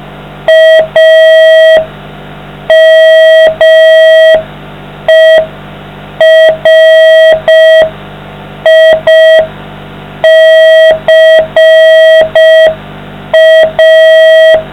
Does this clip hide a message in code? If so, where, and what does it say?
Yes: Morse code from 0.5 to 14.6 s, heard as a keyed tone, about 5 wpm, 640 Hz. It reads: AMERICA